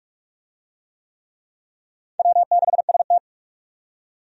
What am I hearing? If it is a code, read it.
Morse code, heard as a keyed tone, about 46 wpm, 700 Hz. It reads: W6ST
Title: USA Call Signs - 4 Characters - 45wpm